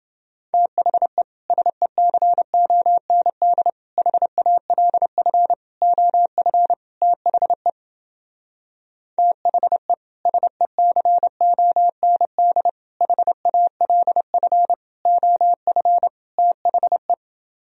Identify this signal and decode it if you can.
Morse code, heard as a keyed tone, about 30 wpm, 700 Hz. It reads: THE SECOND HALF OF THE THE SECOND HALF OF THE